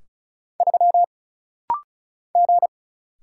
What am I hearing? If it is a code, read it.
Morse code, heard as a keyed tone, about 35 wpm, 700 Hz. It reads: 3 G
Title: Single Letter-Number - Rapid-Fire 35wpm